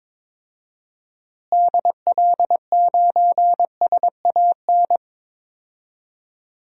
Morse code, heard as a keyed tone, about 22 wpm, 700 Hz. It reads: DL9SAN
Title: DXCC-100 Call Signs - 6 Characters - Repeated 2x - 22wpm